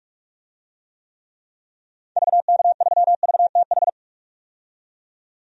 Morse code, heard as a keyed tone, about 45 wpm, 700 Hz. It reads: VK3VTH
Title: DXCC-100 Call Signs - 6 Characters - 45wpm